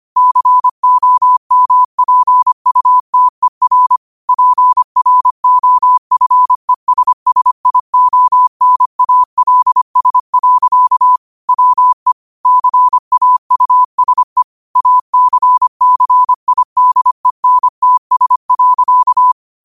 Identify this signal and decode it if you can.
Morse code, heard as a keyed tone, about 25 wpm, 1000 Hz. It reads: COMPUTER PROFESSIONALS. WE CAUSE ACCIDENTS.